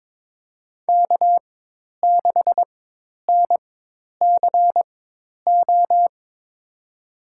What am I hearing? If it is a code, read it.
Morse code, heard as a keyed tone, about 22 wpm, spaced as for 10 wpm, 700 Hz. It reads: K6NCO
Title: USA Call Signs - 5 Characters - 4x Character Spacing - 22wpm